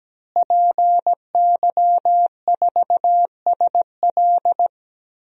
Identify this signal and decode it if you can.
Morse code, heard as a keyed tone, about 17 wpm, 700 Hz. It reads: PY4SL
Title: DXCC-100 Call Signs - 5 Characters - Repeated 3x - 17wpm